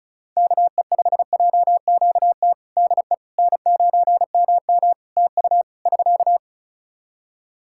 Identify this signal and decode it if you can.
Morse code, heard as a keyed tone, about 36 wpm, 700 Hz. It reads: KE5JQT DE N9MM TU <SK>
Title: K1USN SST QSO Practice 35wpm